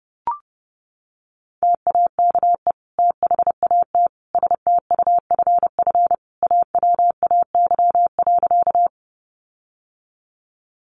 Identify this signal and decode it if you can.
Morse code, heard as a keyed tone, about 30 wpm, 700 Hz. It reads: TAKE THAT STUFF AWAY.
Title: Sentences from Top 1400 Words 30wpm